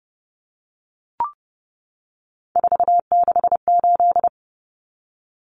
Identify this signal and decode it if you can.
Morse code, heard as a keyed tone, about 30 wpm, 700 Hz. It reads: 468
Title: Numbers - 3-Digit 30wpm